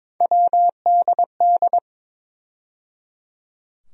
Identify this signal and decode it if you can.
Morse code, heard as a keyed tone, about 22 wpm, 700 Hz. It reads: WDD
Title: USA Call Sign Postfixes - 3 Letters 22wpm